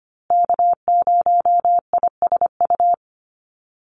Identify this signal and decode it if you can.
Morse code, heard as a keyed tone, about 25 wpm, 700 Hz. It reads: K0ISU
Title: USA Call Signs Repeated 2x 25wpm